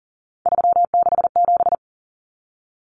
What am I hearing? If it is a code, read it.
Morse code, heard as a keyed tone, about 40 wpm, 700 Hz. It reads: 367
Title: Common Cut Numbers - 3-Digit 40wpm